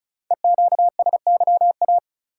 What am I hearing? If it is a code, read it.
Morse code, heard as a keyed tone, about 36 wpm, 700 Hz. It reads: EQSYA
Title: Code Groups - 5 Letters - 35wpm